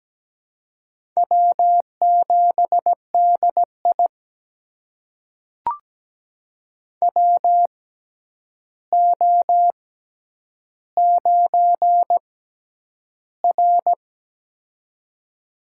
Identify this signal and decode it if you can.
Morse code, heard as a keyed tone, about 17 wpm, spaced as for 9 wpm, 700 Hz. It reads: W7DI WO9R
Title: USA Call Signs - 4 Characters - 6x Character Spacing - 17wpm